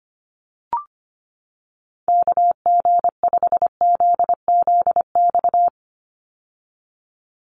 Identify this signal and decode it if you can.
Morse code, heard as a keyed tone, about 25 wpm, 700 Hz. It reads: KG5ZZX
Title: USA Call Signs 25wpm